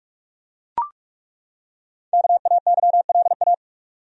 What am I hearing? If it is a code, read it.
Morse code, heard as a keyed tone, about 42 wpm, 700 Hz. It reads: KAYLA
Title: USA Common Names - Max 5 Letters - 45wpm